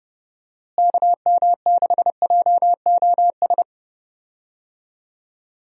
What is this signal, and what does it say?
Morse code, heard as a keyed tone, about 32 wpm, 700 Hz. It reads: KM6JOS
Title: USA Call Signs - 6 Characters - Repeated 3x - 30wpm